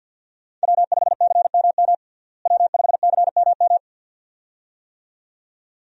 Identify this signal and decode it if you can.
Morse code, heard as a keyed tone, about 50 wpm, 700 Hz. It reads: W5CMM W5CMM
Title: USA Call Signs - 5 Characters - Repeated 2x - 50wpm